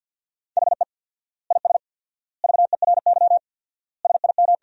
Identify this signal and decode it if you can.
Morse code, heard as a keyed tone, about 47 wpm, 700 Hz. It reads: HE IS VERY SIM
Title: 4x Word-Spacing - Sentences from Top 400 Words - 50wpm